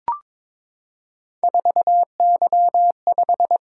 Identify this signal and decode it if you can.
Morse code, heard as a keyed tone, about 22 wpm, 700 Hz. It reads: 4Y5
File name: practice / 22wpm / Three Letter-Number Combinations - 22wpm